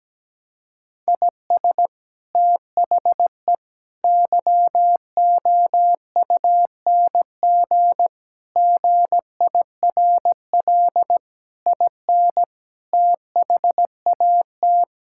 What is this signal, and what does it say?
Morse code, heard as a keyed tone, about 17 wpm, 700 Hz. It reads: IS THE YOUNG GIRL IN THAT
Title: Sentences from Top 1000 Words - Review Part 5 - 17wpm